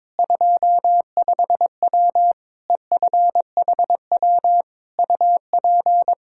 Morse code, heard as a keyed tone, about 22 wpm, 700 Hz. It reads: 25W EFHW UP
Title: Practice QSO - Exchange 2 - Info - 22wpm